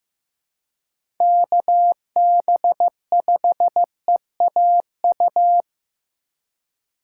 Morse code, heard as a keyed tone, about 15 wpm, 700 Hz. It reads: KB5EAU
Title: USA Call Signs - 6 Characters - 4x Character Spacing - 15wpm